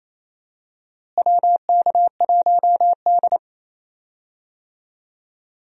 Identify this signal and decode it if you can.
Morse code, heard as a keyed tone, about 28 wpm, 700 Hz. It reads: WK1D
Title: USA Call Signs - 4 Characters - Repeated 2x - 28wpm